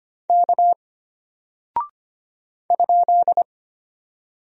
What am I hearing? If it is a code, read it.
Morse code, heard as a keyed tone, about 25 wpm, 700 Hz. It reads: K ?
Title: Single Character (Letter:Number:Punctuation:Prosign) Alt 25wpm